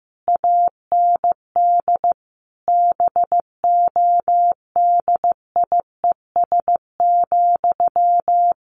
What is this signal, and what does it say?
Morse code, heard as a keyed tone, about 15 wpm, 700 Hz. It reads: AND BODIES,
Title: A Sound Of Thunder 15wpm